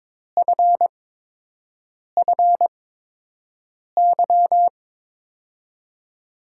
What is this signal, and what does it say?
Morse code, heard as a keyed tone, about 22 wpm, spaced as for 6 wpm, 700 Hz. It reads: FFY
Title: USA Call Signs - 8x Character Spacing - 22wpm